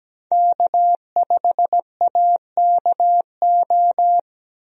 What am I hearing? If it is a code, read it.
Morse code, heard as a keyed tone, about 17 wpm, 700 Hz. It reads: K5AKO
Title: USA Call Signs - 5 Characters - 17wpm